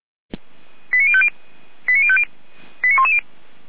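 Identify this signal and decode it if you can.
Five-tone selective call (ZVEI), heard as quick runs of five tones, tones of about 70 ms. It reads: 89050 89050 89110
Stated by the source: ZVEI five-tone call 89294, received by SDR.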